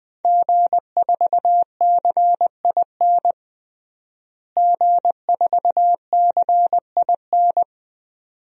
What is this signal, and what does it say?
Morse code, heard as a keyed tone, about 20 wpm, 700 Hz. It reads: G4CIN G4CIN
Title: DXCC-100 Call Signs - 5 Characters - Repeated 2x - 20wpm